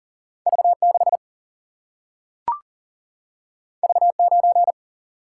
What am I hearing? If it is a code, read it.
Morse code, heard as a keyed tone, about 40 wpm, 700 Hz. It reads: V 6 V 9
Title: Single-Letter Single-Number 40wpm